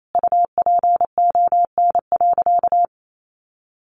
Morse code, heard as a keyed tone, about 28 wpm, 700 Hz. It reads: UPON.
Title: Sentences from Top 1400 Words 28wpm